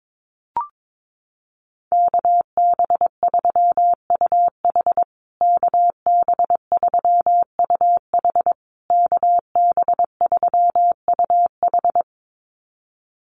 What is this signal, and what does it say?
Morse code, heard as a keyed tone, about 22 wpm, 700 Hz. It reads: KB3UH KB3UH KB3UH
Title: USA Call Signs Repeated 3x 22wpm